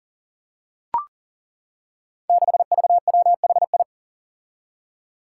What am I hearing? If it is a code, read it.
Morse code, heard as a keyed tone, about 42 wpm, 700 Hz. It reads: 6VWHI
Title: Five Challenging Letter-Number Combinations 40wpm